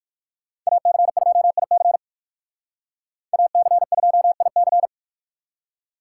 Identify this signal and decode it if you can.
Morse code, heard as a keyed tone, about 59 wpm, 700 Hz. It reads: AC2IC AC2IC